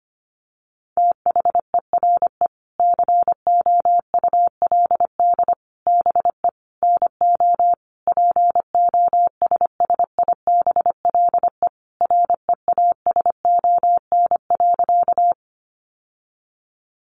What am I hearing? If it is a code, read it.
Morse code, heard as a keyed tone, about 25 wpm, 700 Hz. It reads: THERE COULD BE NO POSSIBLE REASON.
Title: Sentences from Top 700 Words 25wpm